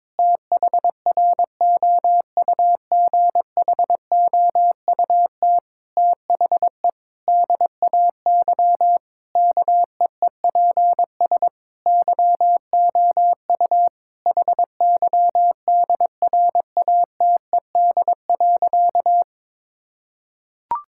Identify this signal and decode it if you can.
Morse code, heard as a keyed tone, about 22 wpm, 700 Hz. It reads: THROUGHOUT THE DAY KEEPS YOU HYDRATED.
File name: Difficult Sentences with Hints - 22wpm